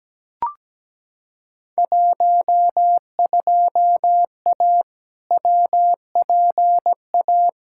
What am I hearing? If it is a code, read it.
Morse code, heard as a keyed tone, about 17 wpm, 700 Hz. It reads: 12A WPA
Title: Field Day Exchanges Encore - 17wpm